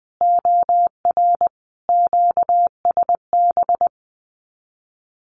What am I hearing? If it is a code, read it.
Morse code, heard as a keyed tone, about 20 wpm, 700 Hz. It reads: OR QSB
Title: Common QSO Phrases 20wpm